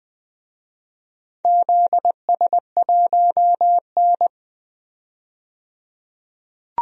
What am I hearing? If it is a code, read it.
Morse code, heard as a keyed tone, about 20 wpm, 700 Hz. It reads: ZS1N
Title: DXCC-100 Call Signs - 4 Characters - 20wpm